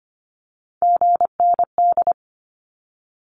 Morse code, heard as a keyed tone, about 25 wpm, 700 Hz. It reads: GND
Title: Abbreviations Repeated 3x - 25wpm